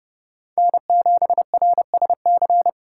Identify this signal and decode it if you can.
Morse code, heard as a keyed tone, about 30 wpm, 700 Hz. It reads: N7RSC